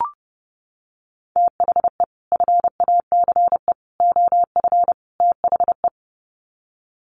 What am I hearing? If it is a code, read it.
Morse code, heard as a keyed tone, about 30 wpm, 700 Hz. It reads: THE FACE OF THE
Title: Sets of 4 Words 30wpm